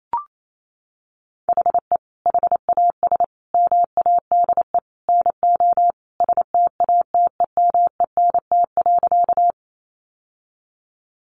Morse code, heard as a keyed tone, about 28 wpm, 700 Hz. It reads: HE HAS MADE NO STATEMENT.
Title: Sentences from Top 1200 Words 28wpm